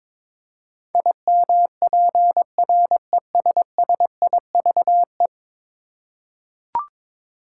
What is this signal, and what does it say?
Morse code, heard as a keyed tone, about 22 wpm, 700 Hz. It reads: IMPRESSIVE
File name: practice / 22wpm / 10 Letter Words - First 3 Letters Given - 22wpm